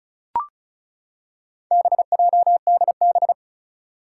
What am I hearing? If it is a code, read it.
Morse code, heard as a keyed tone, about 36 wpm, 700 Hz. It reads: BJDB
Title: Four Challenging Letter-Number Combinations 35wpm